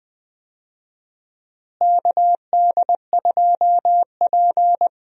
Morse code, heard as a keyed tone, about 20 wpm, 700 Hz. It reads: KD2P
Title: USA Call Signs - 4 Characters - Repeated 3x - 20wpm